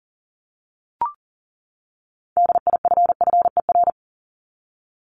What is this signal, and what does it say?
Morse code, heard as a keyed tone, about 40 wpm, 700 Hz. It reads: DIFFER
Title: Top 500 Words 40wpm